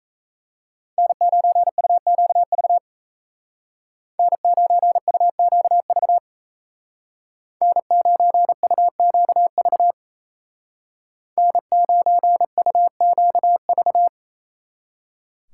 Morse code, heard as a keyed tone, about 36 wpm, 700 Hz. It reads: N9UQV N9UQV N9UQV N9UQV